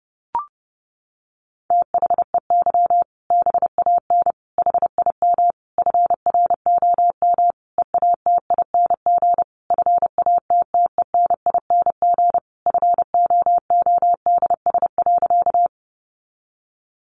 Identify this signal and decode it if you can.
Morse code, heard as a keyed tone, about 30 wpm, 700 Hz. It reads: THEY BAN HIM FROM EATING FATTENING FOODS.
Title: Easy Phrases 30wpm